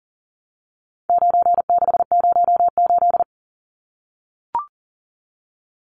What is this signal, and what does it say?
Morse code, heard as a keyed tone, about 40 wpm, 700 Hz. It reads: 9608
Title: Numbers - 4-Digit 40wpm